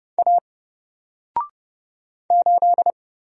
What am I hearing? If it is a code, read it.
Morse code, heard as a keyed tone, about 29 wpm, 700 Hz. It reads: A 8
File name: Single-Letter-Number-30wpm